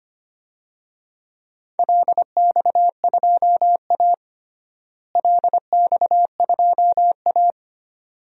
Morse code, heard as a keyed tone, about 25 wpm, 700 Hz. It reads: LX2A LX2A
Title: DXCC-100 Call Signs - 4 Characters - Repeated 2x - 25wpm